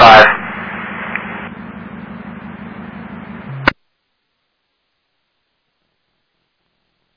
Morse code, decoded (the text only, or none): none